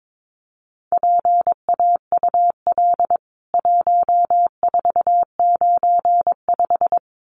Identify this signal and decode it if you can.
Morse code, heard as a keyed tone, about 22 wpm, 700 Hz. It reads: PAUL 1495
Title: CWT Exchange 22wpm